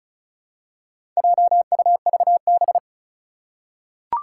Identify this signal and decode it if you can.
Morse code, heard as a keyed tone, about 36 wpm, 700 Hz. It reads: JUVB
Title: Four Challenging Letter-Number Combinations 35wpm